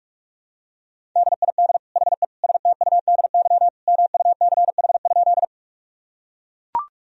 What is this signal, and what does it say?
Morse code, heard as a keyed tone, about 42 wpm, 700 Hz. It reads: DID HE STUDY MUCH?
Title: Sentences from Top 1000 Words - Review Part 1 - 45wpm